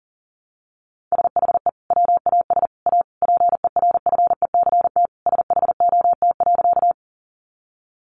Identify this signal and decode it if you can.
Morse code, heard as a keyed tone, about 40 wpm, 700 Hz. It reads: SHE WAS A PERFECT SHOT.
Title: Sentences from Top 1100 Words 40wpm